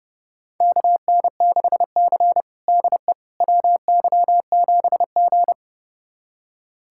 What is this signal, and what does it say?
Morse code, heard as a keyed tone, about 32 wpm, 700 Hz. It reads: KN6C DE WY7G
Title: USA Callsign DE Callsign - 4 Characters - 30wpm